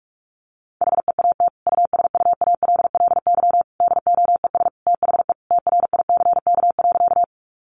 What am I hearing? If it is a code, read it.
Morse code, heard as a keyed tone, about 44 wpm, 700 Hz. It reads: HEAT USUALLY DOES THE TRICK.
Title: Sentences from Top 1500 Words 45wpm